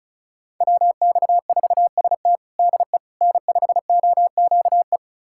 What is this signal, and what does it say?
Morse code, heard as a keyed tone, about 36 wpm, 700 Hz. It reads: WX4ST DE N5OQE
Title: USA Callsign DE Callsign 35wpm